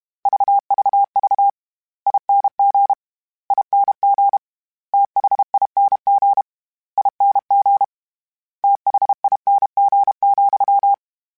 Morse code, heard as a keyed tone, about 32 wpm, 800 Hz. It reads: VVV ING ING THING ING THING,
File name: Special INGs / Special INGs-mp3s / VerySlowToFaster Ing @30